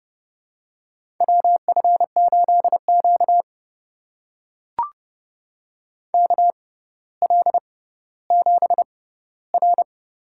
Morse code, heard as a keyed tone, about 30 wpm, spaced as for 15 wpm, 700 Hz. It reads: WF8Q KL7R
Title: USA Call Signs - 4 Characters - 6x Character Spacing - 30wpm